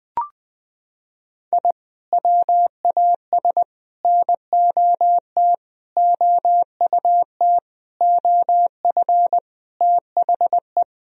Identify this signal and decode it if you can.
Morse code, heard as a keyed tone, about 20 wpm, 700 Hz. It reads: I WAS NOT OUT OF THE
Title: Sentences from Top 1000 Words - Review Part 2 - 20wpm